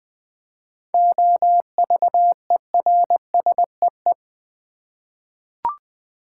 Morse code, heard as a keyed tone, about 20 wpm, 700 Hz. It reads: OVERSEE
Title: 7 Letter Words - First 3 Letters Given - 20wpm